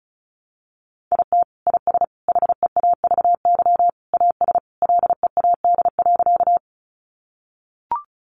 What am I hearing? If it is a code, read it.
Morse code, heard as a keyed tone, about 34 wpm, 700 Hz. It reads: IT IS HEAVY AS LEAD.
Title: Sentences from Top 1100 Words 35wpm